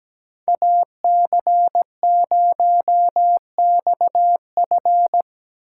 Morse code, heard as a keyed tone, about 19 wpm, 700 Hz. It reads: AC0XF
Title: Speed Racing - USA 2x2 Call Signs 17wpm